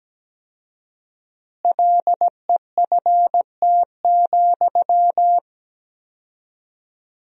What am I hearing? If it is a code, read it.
Morse code, heard as a keyed tone, about 17 wpm, 700 Hz. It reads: LEFT,
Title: 4x Word-Spacing - Sentences from Top 400 Words - 17wpm